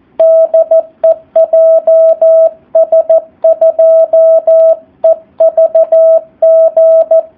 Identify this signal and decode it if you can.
Morse code, heard as a keyed tone, about 14 wpm, 650 Hz, with radio noise. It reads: DE JS2EVG
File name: jh8cbh / 2019cw / SP04- w